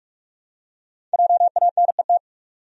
Morse code, heard as a keyed tone, about 47 wpm, 700 Hz. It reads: JANET
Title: USA Common Names - Max 5 Letters - 45wpm